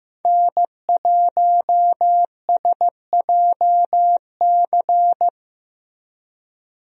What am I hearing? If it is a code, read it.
Morse code, heard as a keyed tone, about 15 wpm, 700 Hz. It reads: N1SJC
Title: USA Call Signs - 5 Characters - 15wpm